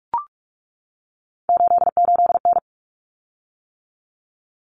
Morse code, heard as a keyed tone, about 44 wpm, 700 Hz. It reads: 88N
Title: Common Cut Numbers - 3-Digit 45wpm